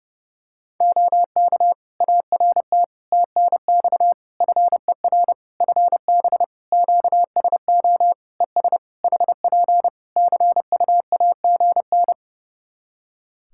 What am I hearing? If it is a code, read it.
Morse code, heard as a keyed tone, about 30 wpm, 700 Hz. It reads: OK ART TNX FER FB QSO ES HP CUAGN
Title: Practice QSO - Exchange 4 - Ending - 30wpm